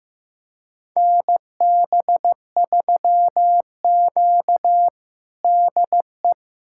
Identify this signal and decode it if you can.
Morse code, heard as a keyed tone, about 16 wpm, 700 Hz. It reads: NB3Q DE
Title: USA Callsign DE Callsign - 4 Characters - 15wpm